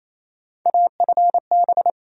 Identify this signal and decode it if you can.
Morse code, heard as a keyed tone, about 28 wpm, 700 Hz. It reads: AFB
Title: USA Call Sign Postfixes - 3 Letters 28wpm